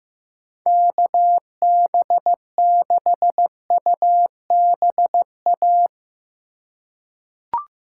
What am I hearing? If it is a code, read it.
Morse code, heard as a keyed tone, about 15 wpm, 700 Hz. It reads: KB6UBA
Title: USA Call Signs - 3x Character Spacing - 15wpm